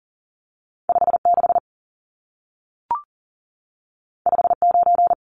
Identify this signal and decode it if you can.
Morse code, heard as a keyed tone, about 40 wpm, 700 Hz. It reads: 56 59
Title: Numbers - 2-Digit 40wpm